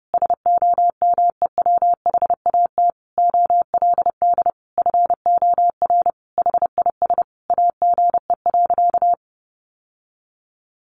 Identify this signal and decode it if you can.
Morse code, heard as a keyed tone, about 30 wpm, 700 Hz. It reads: SOMEWHAT OLD FOR HIS AGE.
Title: Sentences from Top 1300 Words 30wpm